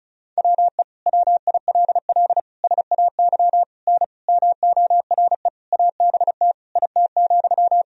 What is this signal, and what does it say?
Morse code, heard as a keyed tone, about 36 wpm, 700 Hz. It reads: WE WILL SAY N MORE ABT IT,
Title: Sentences from Top 200 Words and Abbreviations 35wpm